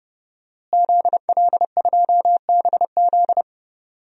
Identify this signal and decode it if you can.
Morse code, heard as a keyed tone, about 30 wpm, 700 Hz. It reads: ZL2BZ
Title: DXCC-100 Call Signs - 5 Characters - Repeated 3x - 30wpm